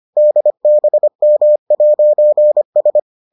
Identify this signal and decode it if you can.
Morse code, heard as a keyed tone, about 25 wpm, 600 Hz. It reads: DBM'S